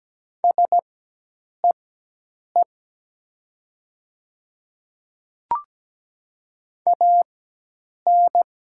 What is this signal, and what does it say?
Morse code, heard as a keyed tone, about 17 wpm, spaced as for 8 wpm, 700 Hz. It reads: SEE AN
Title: 4x Character Spacing - Top 100 Words - 17wpm